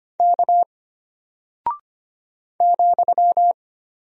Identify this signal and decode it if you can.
Morse code, heard as a keyed tone, about 25 wpm, 700 Hz. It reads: K ,
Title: Single Character (Letter:Number:Punctuation:Prosign) Alt 25wpm